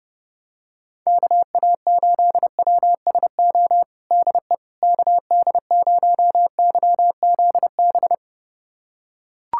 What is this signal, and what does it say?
Morse code, heard as a keyed tone, about 30 wpm, 700 Hz. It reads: KA8WSO DE KD0YZB
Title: USA Callsign DE Callsign - 6 Characters - 30wpm